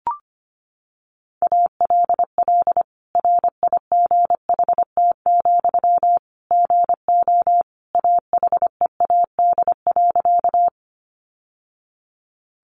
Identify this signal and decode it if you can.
Morse code, heard as a keyed tone, about 25 wpm, 700 Hz. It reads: ALL RIGHT, GO AHEAD.
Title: Sentences from Top 1100 Words 25wpm